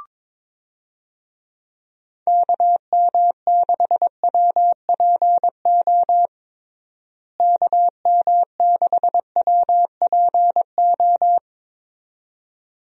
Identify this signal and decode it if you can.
Morse code, heard as a keyed tone, about 22 wpm, 700 Hz. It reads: KM6WPO KM6WPO
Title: USA Call Signs - 6 Characters - Repeated 2x - 22wpm